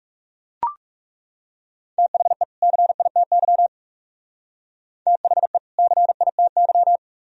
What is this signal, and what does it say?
Morse code, heard as a keyed tone, about 35 wpm, 700 Hz. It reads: THE CITY THE CITY
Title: Speed-Racing - Sets of 2 Words 30wpm